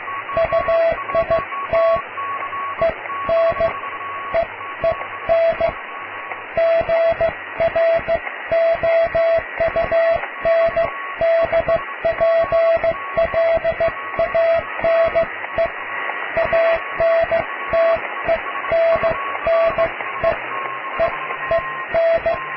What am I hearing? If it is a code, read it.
Morse code, heard as a keyed tone, about 15 wpm, spaced as for 13 wpm, 660 Hz, with radio noise. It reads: UIT EN EEN GROUNDPLANE ANTENNE EEN
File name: cw / tekst(a)+QRM